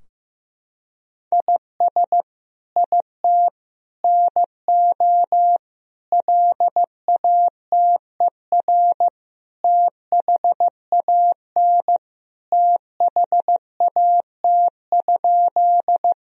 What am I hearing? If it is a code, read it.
Morse code, heard as a keyed tone, about 16 wpm, 700 Hz. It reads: IS IT NO LATER THAN THAT?
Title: Sentences from Top 1000 Words - Review Part 4 - 15wpm